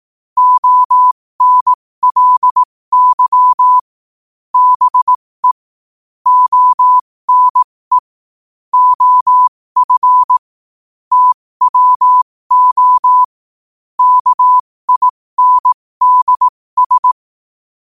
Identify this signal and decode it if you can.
Morse code, heard as a keyed tone, about 18 wpm, spaced as for 15 wpm, 1000 Hz. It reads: ONLY BE ONE OF TWO KINDS
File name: qotdfiles / 15wpm